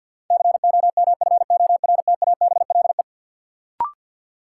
Morse code, heard as a keyed tone, about 49 wpm, 700 Hz. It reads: COMFORTABLE
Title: Sentences from Top 1500 Words - New Word Drill Alt - 50wpm